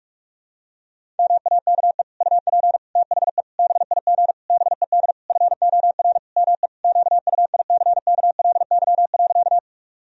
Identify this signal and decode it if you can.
Morse code, heard as a keyed tone, about 46 wpm, 700 Hz. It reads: MAKE UP THE BIG BED FOR ME QUICKLY.
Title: Sentences from Top 1000 Words - Review Part 1 - 45wpm